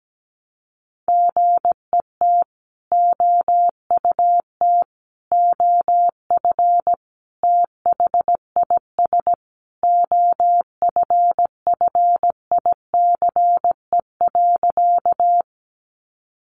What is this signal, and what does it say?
Morse code, heard as a keyed tone, about 17 wpm, 700 Hz. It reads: GET OUT OF THIS OFFICE.
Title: Sentences from Top 1000 Words 17wpm